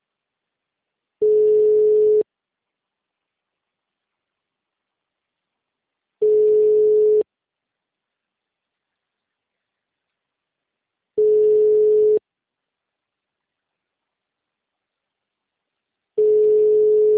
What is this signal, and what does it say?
Ringback tone (Europe, 425 Hz).
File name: out- -2024- - - .28184